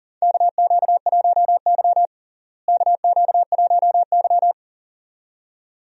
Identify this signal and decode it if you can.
Morse code, heard as a keyed tone, about 40 wpm, 700 Hz. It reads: KQ1Y KQ1Y
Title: USA Call Signs - 4 Characters - Repeated 2x - 40wpm